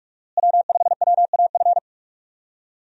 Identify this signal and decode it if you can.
Morse code, heard as a keyed tone, about 42 wpm, 700 Hz. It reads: W5WAF
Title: USA Call Signs - 5 Characters - Repeated 3x - 45wpm